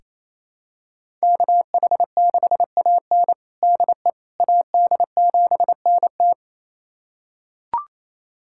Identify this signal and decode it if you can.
Morse code, heard as a keyed tone, about 28 wpm, 700 Hz. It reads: KH6AN DE AD7NT
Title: USA Callsign DE Callsign 28wpm